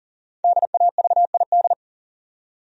Morse code, heard as a keyed tone, about 40 wpm, 700 Hz. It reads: DAVID
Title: USA Common Names - Max 5 Letters - 40wpm